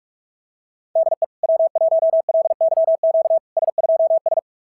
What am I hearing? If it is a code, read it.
Morse code, heard as a keyed tone, about 41 wpm, 690 Hz. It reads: DE W1LYQ S2S
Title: SOTA QSO Practice 45wpm